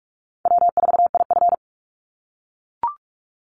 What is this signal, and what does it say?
Morse code, heard as a keyed tone, about 44 wpm, 700 Hz. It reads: W4IF
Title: Call Signs - 45wpm